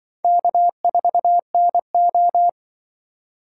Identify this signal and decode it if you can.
Morse code, heard as a keyed tone, about 24 wpm, 700 Hz. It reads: K4NO
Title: Speed Racing - USA 1x2 Call Signs 20wpm